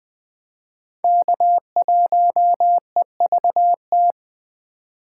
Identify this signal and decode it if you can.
Morse code, heard as a keyed tone, about 23 wpm, 700 Hz. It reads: K1EVT
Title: Speed Racing - USA 1x3 Call Signs 20wpm